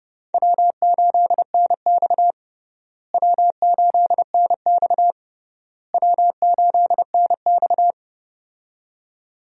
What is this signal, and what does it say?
Morse code, heard as a keyed tone, about 32 wpm, 700 Hz. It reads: W8NX W8NX W8NX
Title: USA Call Signs - 4 Characters - Repeated 3x - 30wpm